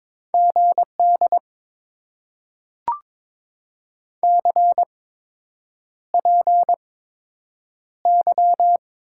Morse code, heard as a keyed tone, about 22 wpm, spaced as for 8 wpm, 700 Hz. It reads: GD CPY
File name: Top 90 QSO Elements - 8x Character Spacing - 22wpm